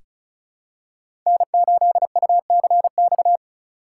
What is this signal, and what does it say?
Morse code, heard as a keyed tone, about 36 wpm, 700 Hz. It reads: N8UCX
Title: USA Call Signs - 5 Characters - 2x Character Spacing - 35wpm